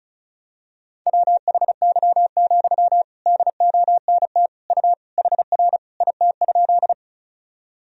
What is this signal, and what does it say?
Morse code, heard as a keyed tone, about 36 wpm, 700 Hz. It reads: WHY, DONT U HR IT?
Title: Sentences from Top 300 Words and Abbreviations 35wpm